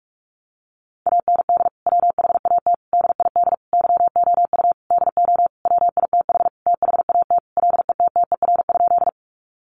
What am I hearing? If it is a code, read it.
Morse code, heard as a keyed tone, about 44 wpm, 700 Hz. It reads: AND WHAT DID YOU DO WITH THAT LETTER?
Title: Sentences from Top 300 Words 45wpm